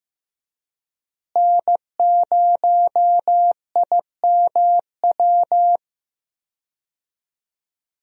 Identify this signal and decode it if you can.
Morse code, heard as a keyed tone, about 15 wpm, 700 Hz. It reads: N0IMW
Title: USA Call Signs - 5 Characters - Repeated 2x - 15wpm